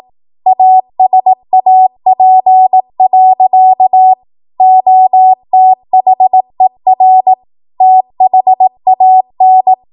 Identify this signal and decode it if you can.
Morse code, heard as a keyed tone, about 18 wpm, 750 Hz. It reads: ASAP. OTHER THAN
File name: Archive / 18 WPM / 18WPM